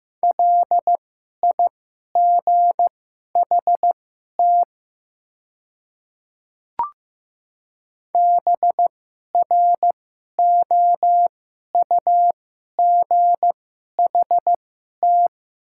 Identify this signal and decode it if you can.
Morse code, heard as a keyed tone, about 15 wpm, spaced as for 11 wpm, 700 Hz. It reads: LIGHT BROUGHT
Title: 2x Character Spacing - Top 500 Words - 15wpm